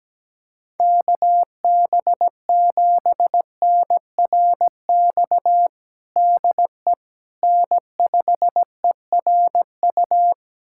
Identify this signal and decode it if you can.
Morse code, heard as a keyed tone, about 17 wpm, 700 Hz. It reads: KB7NRX DE N5ERU
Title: USA Callsign DE Callsign 17wpm